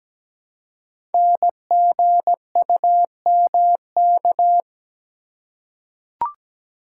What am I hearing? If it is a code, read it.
Morse code, heard as a keyed tone, about 17 wpm, 700 Hz. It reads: NGUMK
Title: Code Groups - 5 Letters - 17wpm